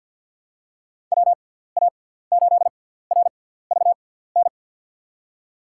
Morse code, heard as a keyed tone, about 52 wpm, spaced as for 17 wpm, 700 Hz. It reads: WA8RVN
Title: USA Call Signs - 6x Character Spacing - 50wpm